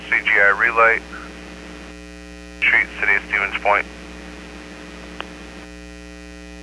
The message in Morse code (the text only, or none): none